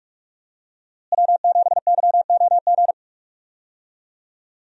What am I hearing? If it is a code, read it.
Morse code, heard as a keyed tone, about 42 wpm, 700 Hz. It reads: W7YOG
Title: USA Call Signs - 5 Characters - Repeated 3x - 45wpm